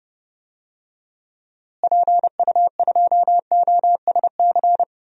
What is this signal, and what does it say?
Morse code, heard as a keyed tone, about 32 wpm, 700 Hz. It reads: PU2OSC
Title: DXCC-100 Call Signs - 6 Characters - Repeated 2x - 30wpm